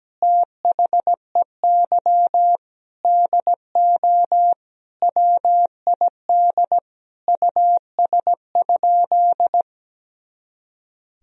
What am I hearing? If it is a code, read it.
Morse code, heard as a keyed tone, about 17 wpm, 700 Hz. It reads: THEY DO WID US?
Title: Sentences from Top 200 Words and Abbreviations 17wpm